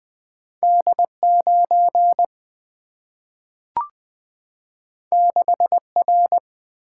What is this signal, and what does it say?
Morse code, heard as a keyed tone, about 20 wpm, 700 Hz. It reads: D 9 6 R
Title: Single-Letter Single-Number 20wpm